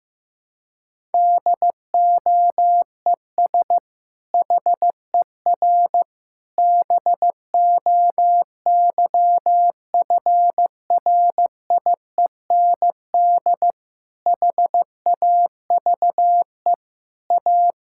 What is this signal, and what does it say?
Morse code, heard as a keyed tone, about 15 wpm, 700 Hz. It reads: DOES HER BOYFRIEND HAVE A
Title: Sentences from Top 1000 Words - Review Part 5 - 15wpm